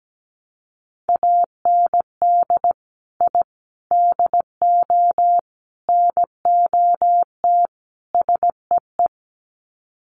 Morse code, heard as a keyed tone, about 17 wpm, 700 Hz. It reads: AND I DO NOT SEE
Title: Sets of 5 Words Encore2 17wpm